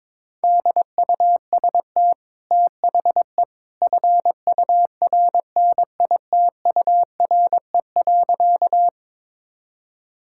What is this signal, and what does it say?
Morse code, heard as a keyed tone, about 22 wpm, 690 Hz. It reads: DUST THE FURNITURE.
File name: practice / 22wpm / Easy Sentences with Hints - 22wpm